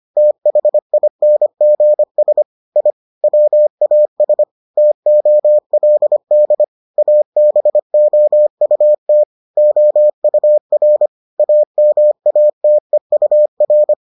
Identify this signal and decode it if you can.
Morse code, heard as a keyed tone, about 25 wpm, 600 Hz. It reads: THINGS I WAS TOLD ABOUT OUR AMATEUR